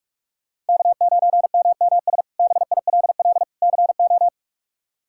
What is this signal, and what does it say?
Morse code, heard as a keyed tone, about 46 wpm, 700 Hz. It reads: K9MMS BILL CO
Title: K1USN SST QSO Practice - 45wpm